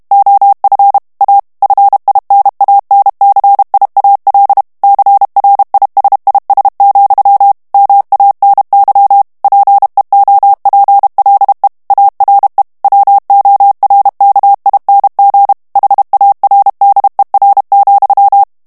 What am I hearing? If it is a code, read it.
Morse code, heard as a keyed tone, about 32 wpm, 800 Hz. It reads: OF A FINANCIAL CRISIS, MANY PEOPLE ARE WORKING HARDER,
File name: morse mp3 / ANGLAIS texte clair 1 vitesse 30 mots minute